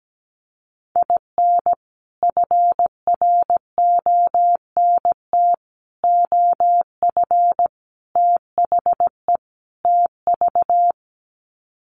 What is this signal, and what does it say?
Morse code, heard as a keyed tone, about 17 wpm, 700 Hz. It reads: IN FRONT OF THE TV
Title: Sets of 5 Words Encore2 17wpm